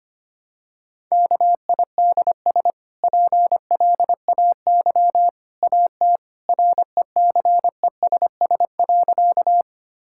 Morse code, heard as a keyed tone, about 25 wpm, 690 Hz. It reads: KIDS PLAY AT RECESS.
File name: Easy Sentences with Hints - 25wpm